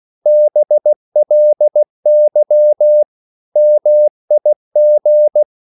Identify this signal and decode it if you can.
Morse code, heard as a keyed tone, about 16 wpm, 600 Hz. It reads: BLY MIG
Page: Korte danske ord 16wpm